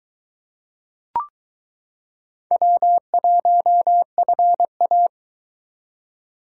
Morse code, heard as a keyed tone, about 21 wpm, 700 Hz. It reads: W1FA